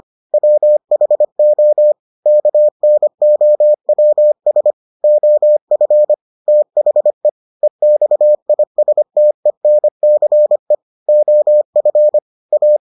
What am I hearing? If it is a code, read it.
Morse code, heard as a keyed tone, about 25 wpm, 600 Hz. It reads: WHO KNOWS OF THE EXISTENCE OF A